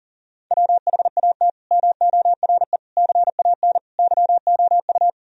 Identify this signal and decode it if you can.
Morse code, heard as a keyed tone, about 40 wpm, 700 Hz. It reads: WHAT MORE CAN YOU
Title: Sentences from Top 1000 Words - Review Part 4 - 40wpm